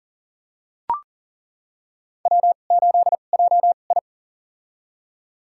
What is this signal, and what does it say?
Morse code, heard as a keyed tone, about 37 wpm, spaced as for 28 wpm, 700 Hz. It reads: W8JI
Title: USA Call Signs - 2x Character Spacing - 40wpm